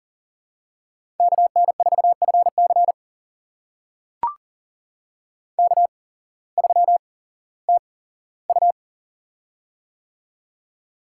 Morse code, heard as a keyed tone, about 39 wpm, spaced as for 19 wpm, 700 Hz. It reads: KN4FC K3TU
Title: USA Call Signs - 8x Character Spacing - 40wpm